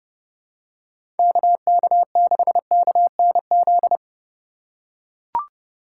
Morse code, heard as a keyed tone, about 32 wpm, 700 Hz. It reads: KK6KNZ
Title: USA Call Signs - 6 Characters - 4x Character Spacing - 30wpm